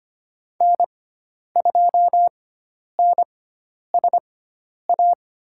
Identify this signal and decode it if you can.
Morse code, heard as a keyed tone, about 25 wpm, spaced as for 10 wpm, 700 Hz. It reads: N2NSA
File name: Lesson 16g - Letter c - Call Signs 25wpm